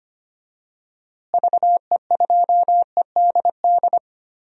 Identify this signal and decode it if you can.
Morse code, heard as a keyed tone, about 25 wpm, 700 Hz. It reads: VE2EDD
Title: DXCC-100 Call Signs - 6 Characters - Repeated 2x - 25wpm